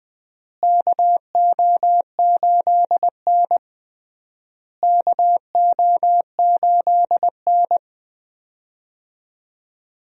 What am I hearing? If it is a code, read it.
Morse code, heard as a keyed tone, about 20 wpm, 700 Hz. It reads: KO8N KO8N
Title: USA Call Signs - 4 Characters - Repeated 3x - 20wpm